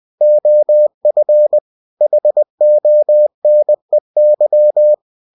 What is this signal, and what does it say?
Morse code, heard as a keyed tone, about 20 wpm, 600 Hz. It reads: OF HONEY
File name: cw-books / winnie-the-pooh / winnie-20wpm-page-0039